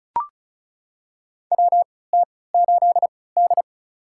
Morse code, heard as a keyed tone, about 34 wpm, spaced as for 20 wpm, 700 Hz. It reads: WT8D
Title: USA Call Signs - 4 Characters - 3x Character Spacing - 35wpm